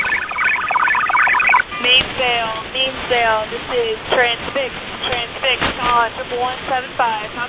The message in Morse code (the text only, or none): none